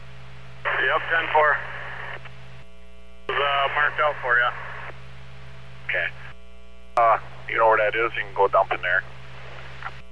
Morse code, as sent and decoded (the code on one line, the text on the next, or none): none